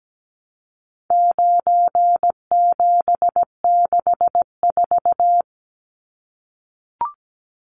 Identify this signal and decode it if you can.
Morse code, heard as a keyed tone, about 17 wpm, 700 Hz. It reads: 9764
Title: Numbers - 4-Digit 17wpm